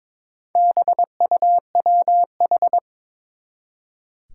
Morse code, heard as a keyed tone, about 22 wpm, 700 Hz. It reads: BUWH
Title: Four Challenging Letter-Number Combinations 22wpm